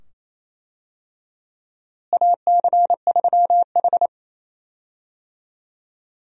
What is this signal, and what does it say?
Morse code, heard as a keyed tone, about 28 wpm, 700 Hz. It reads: AC3H
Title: USA Call Signs - 4 Characters - Repeated 2x - 28wpm